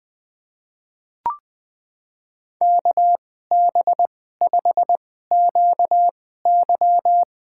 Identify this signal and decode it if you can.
Morse code, heard as a keyed tone, about 20 wpm, spaced as for 14 wpm, 700 Hz. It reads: KB5QY
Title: USA Call Signs - 2x Character Spacing - 20wpm